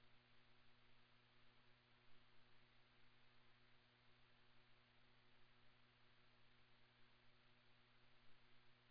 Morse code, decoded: none

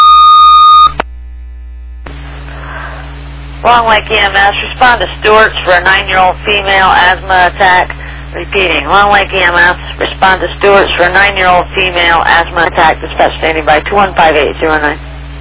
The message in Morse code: none

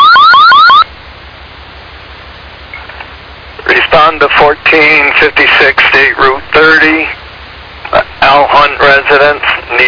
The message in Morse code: none

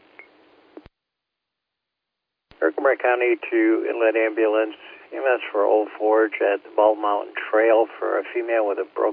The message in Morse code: none

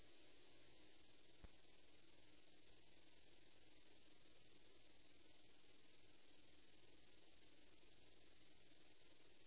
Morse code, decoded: none

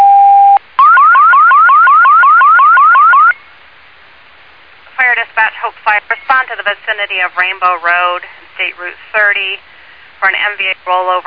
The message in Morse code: none

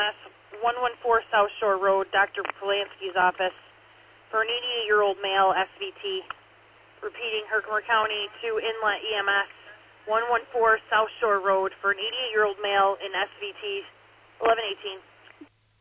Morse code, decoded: none

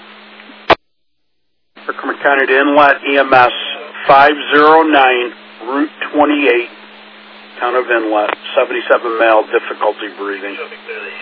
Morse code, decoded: none

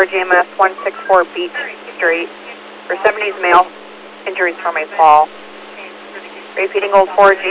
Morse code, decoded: none

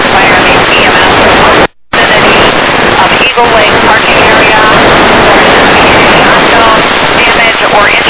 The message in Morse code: none

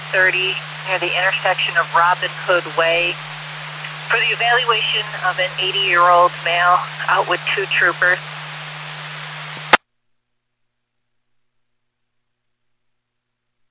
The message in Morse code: none

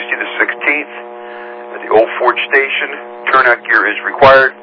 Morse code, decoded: none

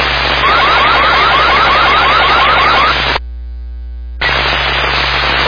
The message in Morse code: none

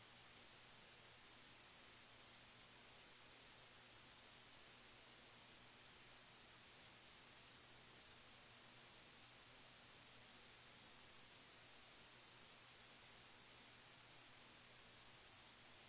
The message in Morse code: none

